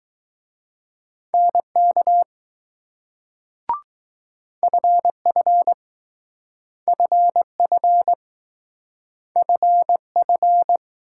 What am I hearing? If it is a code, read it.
Morse code, heard as a keyed tone, about 19 wpm, 700 Hz. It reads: NK FF FF FF